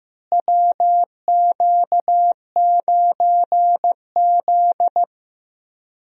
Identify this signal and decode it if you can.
Morse code, heard as a keyed tone, about 15 wpm, 700 Hz. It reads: WQ9Z
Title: USA Call Signs - 4 Characters - Repeated 3x - 15wpm